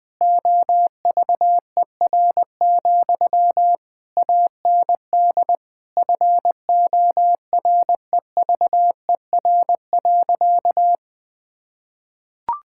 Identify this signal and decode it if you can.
Morse code, heard as a keyed tone, about 20 wpm, 700 Hz. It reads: OVER, AND FOREVER.
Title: Sentences from Top 1600 Words 20wpm